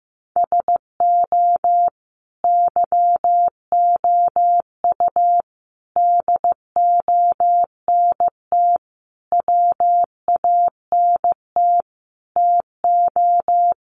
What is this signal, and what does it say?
Morse code, heard as a keyed tone, about 15 wpm, 700 Hz. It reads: SO YOU DONT WANT TO
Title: Sentences from Top 1000 Words 15wpm